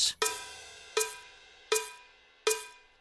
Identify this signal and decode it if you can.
There is no code.